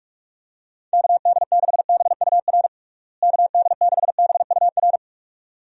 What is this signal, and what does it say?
Morse code, heard as a keyed tone, about 42 wpm, 700 Hz. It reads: KD6BUR KD6BUR
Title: USA Call Signs - 6 Characters - Repeated 2x - 45wpm